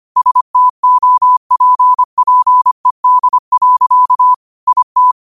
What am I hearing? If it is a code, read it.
Morse code, heard as a keyed tone, about 25 wpm, 1000 Hz. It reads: STOPPED. IT
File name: qotdfiles / 25wpm